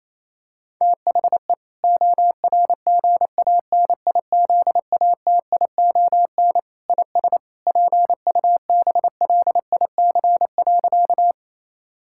Morse code, heard as a keyed tone, about 28 wpm, 700 Hz. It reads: THE ORGANIZATION IS PUBLIC.
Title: Sentences from Top 1000 Words - Review Part 5 - 28wpm